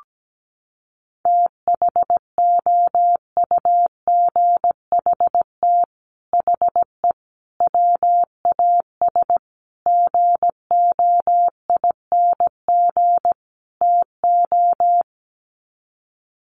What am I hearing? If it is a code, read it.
Morse code, heard as a keyed tone, about 17 wpm, 700 Hz. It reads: THOUGHT HE WAS GOING TO
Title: Sets of 5 Words Encore2 17wpm